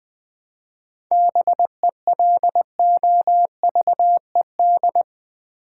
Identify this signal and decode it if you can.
Morse code, heard as a keyed tone, about 20 wpm, 700 Hz. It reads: BELOVED
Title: 7 Letter Words - First 3 Letters Given - 20wpm